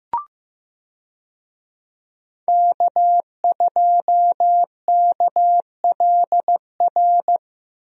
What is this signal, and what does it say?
Morse code, heard as a keyed tone, about 15 wpm, 700 Hz. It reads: K2KLR
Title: USA Call Signs - 5 Characters - 15wpm